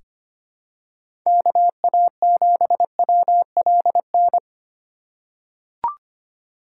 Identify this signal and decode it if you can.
Morse code, heard as a keyed tone, about 25 wpm, 700 Hz. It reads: KA7WLN
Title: USA Call Signs - 3x Character Spacing - 25wpm